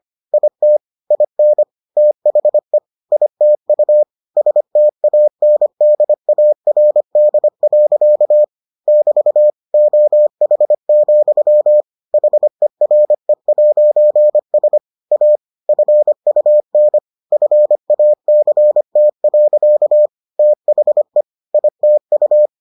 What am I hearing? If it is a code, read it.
Morse code, heard as a keyed tone, about 25 wpm, 600 Hz. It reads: IT IN THE ITU STANDARD. <BT> OH, HERE'S A FUN FACT. THE ITU